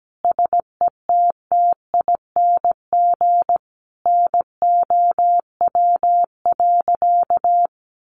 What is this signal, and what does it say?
Morse code, heard as a keyed tone, about 17 wpm, 700 Hz. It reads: SETTING NOW.
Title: Sentences from Top 1300 Words 17wpm